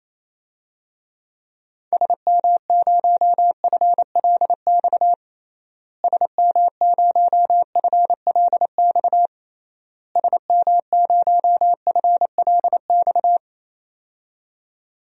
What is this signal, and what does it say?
Morse code, heard as a keyed tone, about 28 wpm, 700 Hz. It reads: SM0FLX SM0FLX SM0FLX
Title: DXCC-100 Call Signs - 6 Characters - Repeated 3x - 28wpm